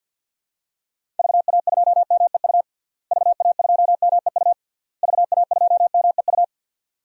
Morse code, heard as a keyed tone, about 52 wpm, 700 Hz. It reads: VA2MEU VA2MEU VA2MEU
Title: DXCC-100 Call Signs - 6 Characters - Repeated 3x - 50wpm